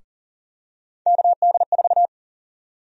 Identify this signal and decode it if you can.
Morse code, heard as a keyed tone, about 40 wpm, 700 Hz. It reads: KD4
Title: USA Call Sign Prefixes - 40wpm